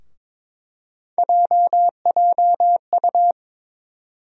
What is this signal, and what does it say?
Morse code, heard as a keyed tone, about 22 wpm, 700 Hz. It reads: JJU
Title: USA Call Sign Postfixes - 3 Letters 22wpm